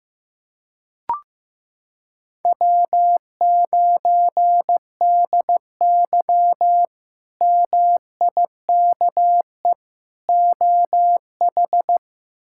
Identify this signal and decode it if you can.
Morse code, heard as a keyed tone, about 16 wpm, 700 Hz. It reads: W9DY MIKE OH
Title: K1USN SST QSO Practice - 15wpm